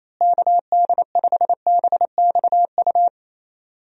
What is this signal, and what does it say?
Morse code, heard as a keyed tone, about 28 wpm, 700 Hz. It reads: KD5BXU
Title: USA Call Signs - 6 Characters - Repeated 3x - 28wpm